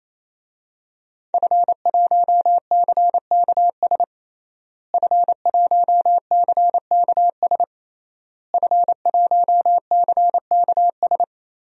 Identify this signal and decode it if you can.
Morse code, heard as a keyed tone, about 28 wpm, 700 Hz. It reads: F1CKS F1CKS F1CKS
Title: DXCC-100 Call Signs - 5 Characters - Repeated 3x - 28wpm